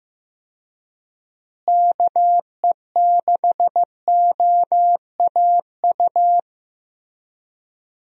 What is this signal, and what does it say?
Morse code, heard as a keyed tone, about 15 wpm, 700 Hz. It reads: KE6OAU
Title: USA Call Signs - 6 Characters - Repeated 3x - 15wpm